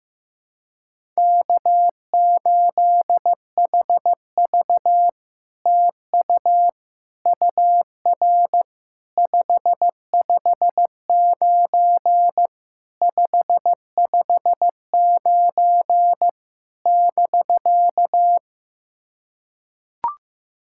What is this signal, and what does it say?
Morse code, heard as a keyed tone, about 15 wpm, 700 Hz. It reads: K8HV TU UR 559 559 <BK>
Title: SOTA QSO Practice 15wpm